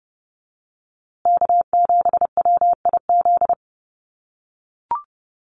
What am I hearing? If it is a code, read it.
Morse code, heard as a keyed tone, about 30 wpm, 700 Hz. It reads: K7WIZ
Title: USA Call Signs Repeated 2x 30wpm